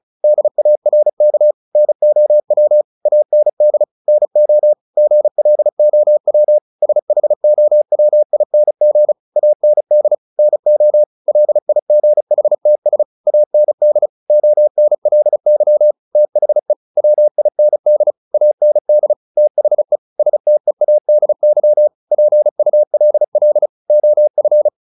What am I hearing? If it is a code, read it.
Morse code, heard as a keyed tone, about 34 wpm, 600 Hz. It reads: DARK NOW AND NO GLOW SHOWING AND NO LIGHTS AND ONLY THE WIND AND THE STEADY PULL OF
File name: the old man and the sea 2 35wpm